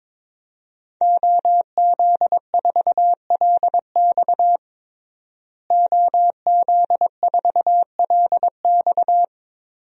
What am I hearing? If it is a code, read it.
Morse code, heard as a keyed tone, about 22 wpm, 700 Hz. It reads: OZ4LX OZ4LX
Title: DXCC-100 Call Signs - 5 Characters - Repeated 2x - 22wpm